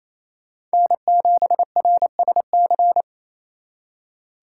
Morse code, heard as a keyed tone, about 36 wpm, 700 Hz. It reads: N7RSC